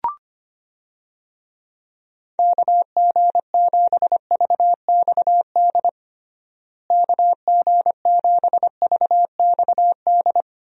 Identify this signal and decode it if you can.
Morse code, heard as a keyed tone, about 25 wpm, 700 Hz. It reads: KG7VXD KG7VXD
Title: USA Call Signs - 6 Characters - Repeated 3x - 25wpm